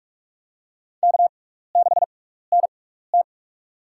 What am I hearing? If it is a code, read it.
Morse code, heard as a keyed tone, about 46 wpm, spaced as for 15 wpm, 700 Hz. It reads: K6NT
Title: USA Call Signs - 6x Character Spacing - 45wpm